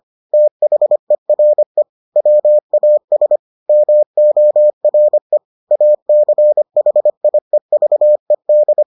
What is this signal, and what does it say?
Morse code, heard as a keyed tone, about 25 wpm, 600 Hz. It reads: THERE WAS MORE ACHIEVED